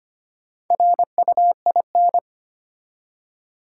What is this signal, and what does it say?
Morse code, heard as a keyed tone, about 25 wpm, 700 Hz. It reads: RUIN
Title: Four Letter Words - 25wpm